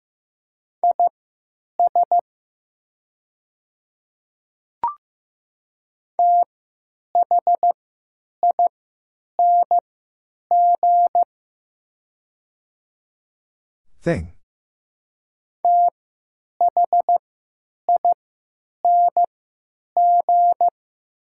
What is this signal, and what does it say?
Morse code, heard as a keyed tone, about 15 wpm, spaced as for 8 wpm, 700 Hz. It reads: IS THING THING
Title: 3x Character Spacing - Top 100 Words - 15wpm